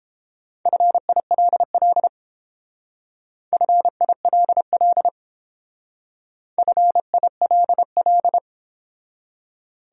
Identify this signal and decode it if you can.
Morse code, heard as a keyed tone, about 29 wpm, 700 Hz. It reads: FILL FILL FILL